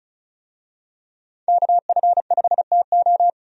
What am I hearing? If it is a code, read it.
Morse code, heard as a keyed tone, about 36 wpm, 700 Hz. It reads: KF5TO
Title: USA Call Signs - 5 Characters - 35wpm